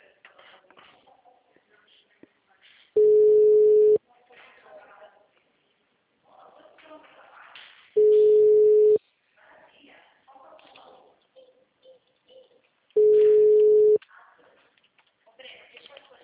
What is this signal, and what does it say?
Ringback tone (Europe, 425 Hz), with speech.
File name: out- -2017- - - .75464